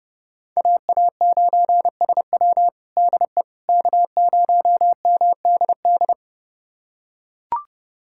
Morse code, heard as a keyed tone, about 30 wpm, 700 Hz. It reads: AA9SW DE K0MDD
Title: USA Callsign DE Callsign 30wpm